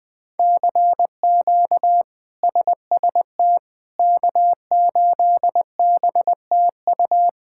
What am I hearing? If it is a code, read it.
Morse code, heard as a keyed tone, about 20 wpm, 700 Hz. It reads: CQ SST K8BTU
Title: K1USN SST QSO Practice - 20wpm